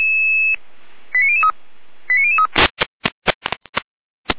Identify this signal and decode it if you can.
Five-tone selective call (ZVEI), heard as quick runs of five tones, tones of about 70 ms. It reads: 89003 89003
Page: /455/SDR/ZVEI/